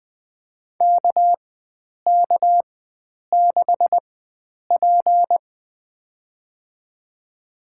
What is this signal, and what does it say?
Morse code, heard as a keyed tone, about 21 wpm, spaced as for 9 wpm, 700 Hz. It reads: KK6P
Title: USA Call Signs - 4 Characters - 4x Character Spacing - 20wpm